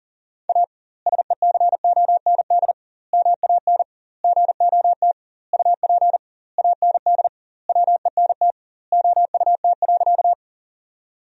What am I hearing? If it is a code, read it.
Morse code, heard as a keyed tone, about 40 wpm, 700 Hz. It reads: A SECOND MAN GOT UP AND WENT OUT.
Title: 2x Word-Spacing - Sentences from Top 300 Words - 40wpm